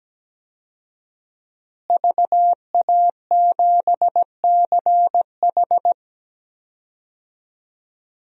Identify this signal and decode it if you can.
Morse code, heard as a keyed tone, about 17 wpm, 700 Hz. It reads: VA7CH
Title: DXCC-100 Call Signs - 5 Characters - Repeated 2x - 17wpm